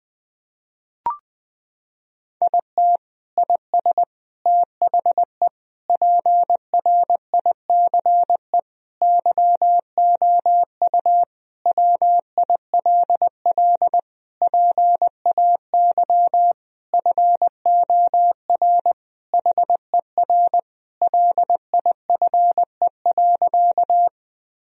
Morse code, heard as a keyed tone, about 21 wpm, 700 Hz. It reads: IT IS THE PRICE YOU WILL PAY FOR HER LIFE.
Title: Sentences from Top 1000 Words - Review Part 4 - 20wpm